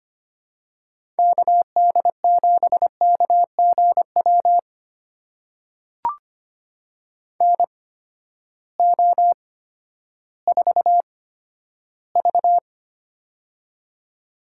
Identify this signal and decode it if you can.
Morse code, heard as a keyed tone, about 25 wpm, spaced as for 12 wpm, 700 Hz. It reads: KD7KGW NO4V
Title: USA Call Signs - 8x Character Spacing - 25wpm